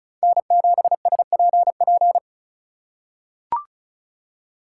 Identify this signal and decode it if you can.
Morse code, heard as a keyed tone, about 36 wpm, 700 Hz. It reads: N7SPP
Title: USA Call Signs - 5 Characters - 2x Character Spacing - 35wpm